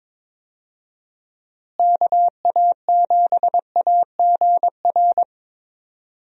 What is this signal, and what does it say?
Morse code, heard as a keyed tone, about 22 wpm, 700 Hz. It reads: KA7AGR
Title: USA Call Signs - 6 Characters - Repeated 2x - 22wpm